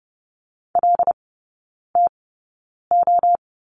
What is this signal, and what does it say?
Morse code, heard as a keyed tone, about 30 wpm, spaced as for 9 wpm, 700 Hz. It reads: LTO
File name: Lesson 17e - Letter u - Three Characters 30wpm